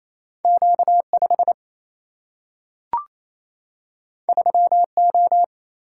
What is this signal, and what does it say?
Morse code, heard as a keyed tone, about 28 wpm, 700 Hz. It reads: Q 5 3 O
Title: Single-Letter Single-Number 28wpm